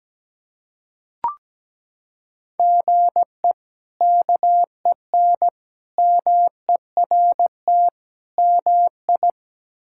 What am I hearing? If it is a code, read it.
Morse code, heard as a keyed tone, about 17 wpm, 700 Hz. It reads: GE KEN MERT MI
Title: K1USN SST QSO Practice - 17wpm